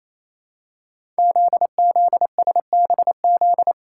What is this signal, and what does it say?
Morse code, heard as a keyed tone, about 28 wpm, 700 Hz. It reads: ZZSBZ
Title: Code Groups - 5 Letters - 28wpm